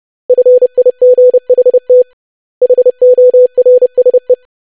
Morse code, heard as a keyed tone, about 30 wpm, 500 Hz. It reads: FIGHT HORSE
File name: lesson / web / basic500-25wpm